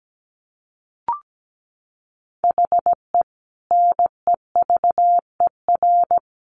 Morse code, heard as a keyed tone, about 17 wpm, 700 Hz. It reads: HE NEVER
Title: Sentences from Top 400 Words 17wpm